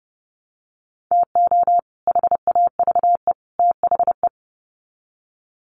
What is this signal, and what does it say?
Morse code, heard as a keyed tone, about 30 wpm, 700 Hz. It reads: TO HAVE THE
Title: Sets of 3 Words 30wpm